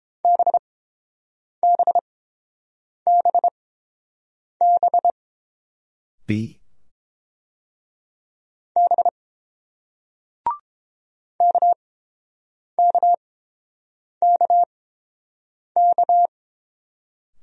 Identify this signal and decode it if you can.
Morse code, heard as a keyed tone, about 29 wpm, spaced as for 7 wpm, 700 Hz. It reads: BBBB B KKKK